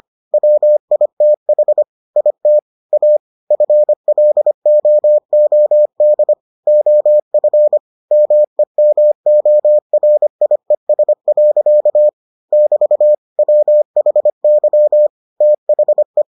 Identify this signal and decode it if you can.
Morse code, heard as a keyed tone, about 25 wpm, 600 Hz. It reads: WITH IT A FLOOD OF MEMORIES. <BT> WHY THE